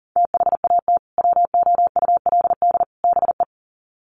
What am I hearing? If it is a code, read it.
Morse code, heard as a keyed tone, about 40 wpm, 700 Hz. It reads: THAT WOULD BE
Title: Sets of 3 Words 40wpm